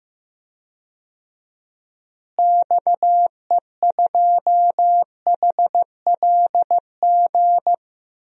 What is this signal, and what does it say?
Morse code, heard as a keyed tone, about 16 wpm, 700 Hz. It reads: XE2HLG
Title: DXCC-100 Call Signs - 6 Characters - Repeated 2x - 15wpm